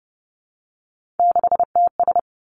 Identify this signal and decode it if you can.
Morse code, heard as a keyed tone, about 32 wpm, 700 Hz. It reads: 6TS
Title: Three Challenging Letter-Number Combinations 30wpm